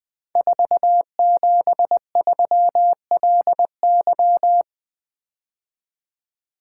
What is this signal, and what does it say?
Morse code, heard as a keyed tone, about 20 wpm, 700 Hz. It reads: 473LY
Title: Code Groups - 5 Letter:Numbers - 20wpm